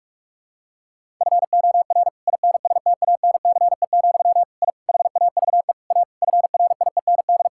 Morse code, heard as a keyed tone, about 46 wpm, 700 Hz. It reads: FOR INSTANCE, I HAVE A FRIEND
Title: Sentences from Top 1000 Words - Review Part 5 - 45wpm